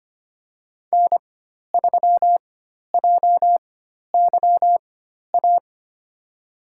Morse code, heard as a keyed tone, about 25 wpm, spaced as for 12 wpm, 700 Hz. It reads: N3JYA
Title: USA Call Signs - 5 Characters - 4x Character Spacing - 25wpm